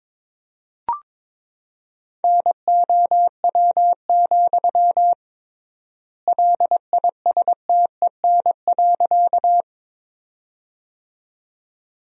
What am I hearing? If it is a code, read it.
Morse code, heard as a keyed tone, about 21 wpm, 700 Hz. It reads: NOW, LISTEN.
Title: 3x Word-Spacing - Sentences from Top 400 Words - 22wpm